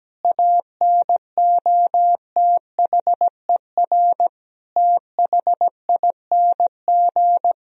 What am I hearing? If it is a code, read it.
Morse code, heard as a keyed tone, about 17 wpm, 700 Hz. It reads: ANOTHER THING
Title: Sentences from Top 1000 Words - Review Part 4 - 17wpm